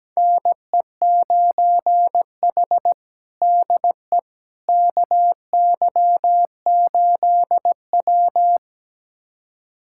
Morse code, heard as a keyed tone, about 17 wpm, 700 Hz. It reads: NE9H DE KY8W
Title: USA Callsign DE Callsign - 4 Characters - 17wpm